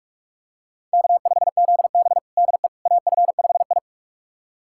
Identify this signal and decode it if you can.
Morse code, heard as a keyed tone, about 46 wpm, 700 Hz. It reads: K5ZB DE AF5I
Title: USA Callsign DE Callsign - 4 Characters - 45wpm